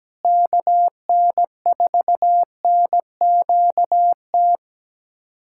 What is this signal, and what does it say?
Morse code, heard as a keyed tone, about 17 wpm, 700 Hz. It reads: KN4NQT
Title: USA Callsign DE Callsign 17wpm